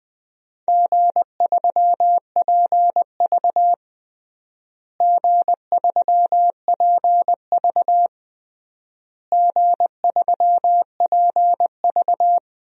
Morse code, heard as a keyed tone, about 21 wpm, 700 Hz. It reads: G3PV G3PV G3PV
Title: DXCC-100 Call Signs - 4 Characters - Repeated 3x - 20wpm